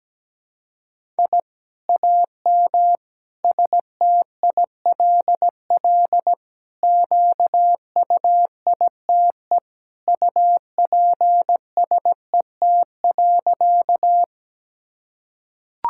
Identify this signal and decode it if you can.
Morse code, heard as a keyed tone, about 17 wpm, 700 Hz. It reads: I AM STILL QUITE UPSET.
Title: Sentences from Top 1000 Words - Review Part 5 - 17wpm